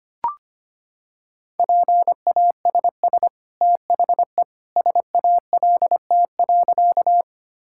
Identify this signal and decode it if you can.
Morse code, heard as a keyed tone, about 25 wpm, 700 Hz. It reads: PASS THE SALT.
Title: Sentences from Top 1000 Words - Review Part 5 - 25wpm